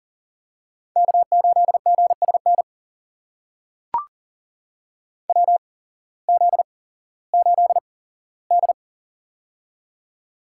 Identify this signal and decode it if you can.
Morse code, heard as a keyed tone, about 40 wpm, spaced as for 19 wpm, 700 Hz. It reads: K8GSN WZ8D
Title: USA Call Signs - 8x Character Spacing - 40wpm